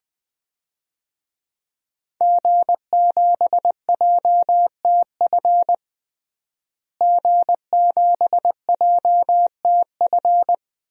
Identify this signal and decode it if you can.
Morse code, heard as a keyed tone, about 20 wpm, 700 Hz. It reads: G7JTF G7JTF
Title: DXCC-100 Call Signs - 5 Characters - Repeated 3x - 20wpm